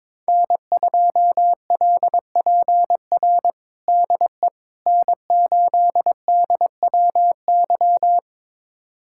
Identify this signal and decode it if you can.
Morse code, heard as a keyed tone, about 22 wpm, 700 Hz. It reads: N2LPR DE N8DWY
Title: USA Callsign DE Callsign - 5 Characters - 22wpm